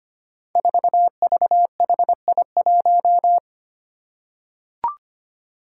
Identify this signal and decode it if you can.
Morse code, heard as a keyed tone, about 25 wpm, 700 Hz. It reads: 4VHI1
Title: Five Challenging Letter-Number Combinations 25wpm